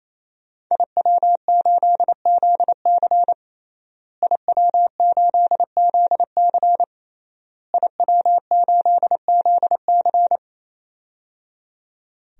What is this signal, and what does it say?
Morse code, heard as a keyed tone, about 28 wpm, 700 Hz. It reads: IW8ZC IW8ZC IW8ZC
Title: DXCC-100 Call Signs - 5 Characters - Repeated 3x - 28wpm